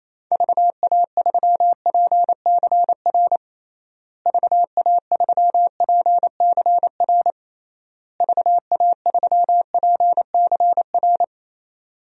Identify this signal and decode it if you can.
Morse code, heard as a keyed tone, about 27 wpm, 700 Hz. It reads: VA3PCR VA3PCR VA3PCR
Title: DXCC-100 Call Signs - 6 Characters - Repeated 3x - 28wpm